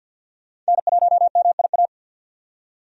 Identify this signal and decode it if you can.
Morse code, heard as a keyed tone, about 45 wpm, 700 Hz. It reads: N1MIA
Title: USA Call Signs - 5 Characters - 50wpm